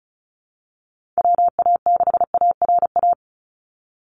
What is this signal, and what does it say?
Morse code, heard as a keyed tone, about 34 wpm, 700 Hz. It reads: WA6ARA
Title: NA SOTA Chaser Callsigns 35wpm